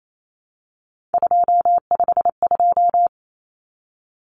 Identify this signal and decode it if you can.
Morse code, heard as a keyed tone, about 28 wpm, 700 Hz. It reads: 252
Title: Common Cut Numbers - 3-Digit 28wpm